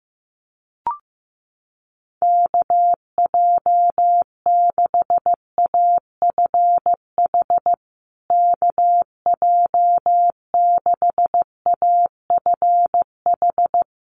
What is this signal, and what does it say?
Morse code, heard as a keyed tone, about 15 wpm, 700 Hz. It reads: KJ6AFH KJ6AFH
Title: USA Call Signs Repeated 3x 15wpm